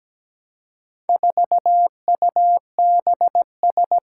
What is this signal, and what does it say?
Morse code, heard as a keyed tone, about 17 wpm, 700 Hz. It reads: 4UBS
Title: Four Challenging Letter-Number Combinations 17wpm